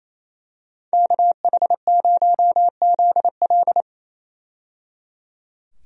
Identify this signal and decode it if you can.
Morse code, heard as a keyed tone, about 28 wpm, 700 Hz. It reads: KH0ZL
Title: USA Call Signs - 5 Characters - 28wpm